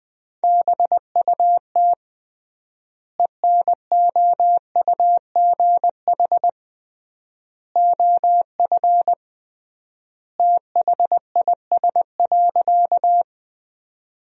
Morse code, heard as a keyed tone, about 20 wpm, 700 Hz. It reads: BUT ENOUGH OF THIS.
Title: 3x Word-Spacing - Sentences from Top 400 Words - 20wpm